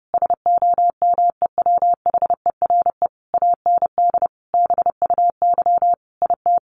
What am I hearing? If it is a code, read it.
Morse code, heard as a keyed tone, about 30 wpm, 700 Hz. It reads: SOMEWHERE AND BUY IT
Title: Sentences from Top 1400 Words 30wpm